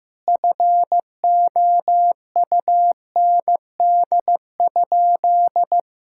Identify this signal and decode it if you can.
Morse code, heard as a keyed tone, about 15 wpm, 700 Hz. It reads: FOUND?
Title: Sentences from Top 1600 Words 15wpm